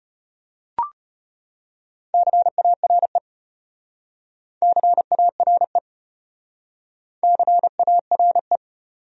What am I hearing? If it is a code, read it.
Morse code, heard as a keyed tone, about 35 wpm, 700 Hz. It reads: CARE CARE CARE